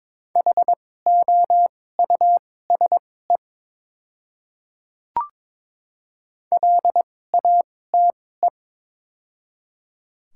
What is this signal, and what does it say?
Morse code, heard as a keyed tone, about 22 wpm, spaced as for 16 wpm, 700 Hz. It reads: HOUSE LATE
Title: 2x Character Spacing - Top 500 Words - 22wpm